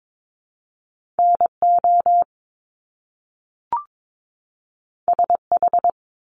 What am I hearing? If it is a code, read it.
Morse code, heard as a keyed tone, about 22 wpm, 700 Hz. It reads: NO SH
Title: Top Two Letter Combinations in Words 22wpm